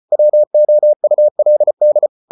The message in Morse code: WOULD

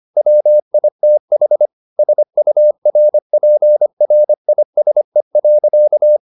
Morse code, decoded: WITH SURPRISE.

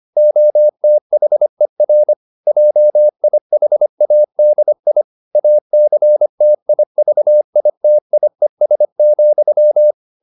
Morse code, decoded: OTHER JIHADI ACTIVITIES,

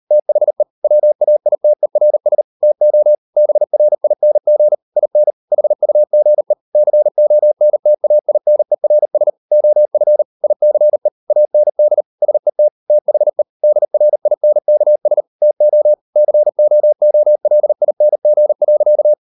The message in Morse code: THE WAITERS TO BRING IN HUGE CONTAINERS OF ICE AND SET THE DRINKS TO COOLING.

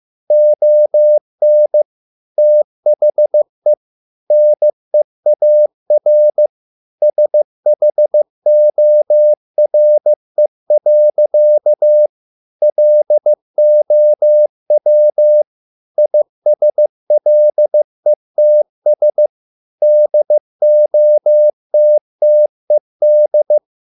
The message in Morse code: ON THE NEAR SHORE. LOW ISLETS DOTTED

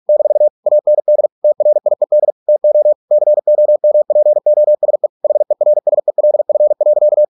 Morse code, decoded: - AND TRIED TO COMPOSE HERSELF.

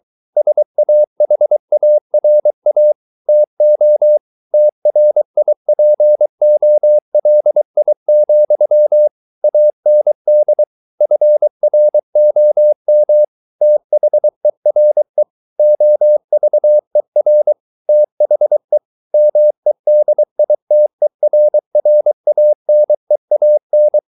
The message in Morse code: SAHARA TO TRIPOLI, AND FROM THERE OVER THE MEDITERRANEAN

